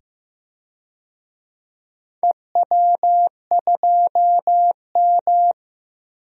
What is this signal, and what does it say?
Morse code, heard as a keyed tone, about 15 wpm, 700 Hz. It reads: EW2M
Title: DXCC-100 Call Signs - 4 Characters - 15wpm